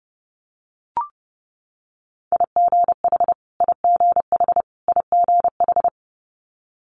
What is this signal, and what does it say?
Morse code, heard as a keyed tone, about 32 wpm, 700 Hz. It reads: IGH IGH IGH
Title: Multi-letter Phonemes Repeated 3x 30wpm